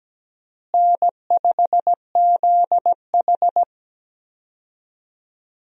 Morse code, heard as a keyed tone, about 17 wpm, 700 Hz. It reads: N5ZH
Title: USA Call Signs - 4 Characters - Repeated 2x - 17wpm